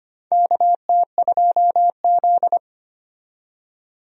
Morse code, heard as a keyed tone, about 25 wpm, 700 Hz. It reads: KT2Z
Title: USA Call Signs - 4 Characters - 25wpm